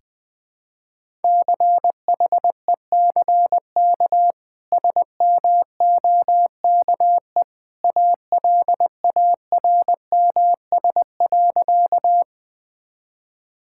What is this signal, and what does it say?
Morse code, heard as a keyed tone, about 20 wpm, 700 Hz. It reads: CHECK SMOKE ALARMS.
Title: Easy Sentences with Hints - 20wpm